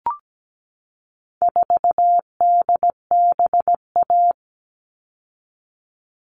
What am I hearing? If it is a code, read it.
Morse code, heard as a keyed tone, about 17 wpm, 700 Hz. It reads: 4DBA
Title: All Cut Numbers - 4-Digit 17wpm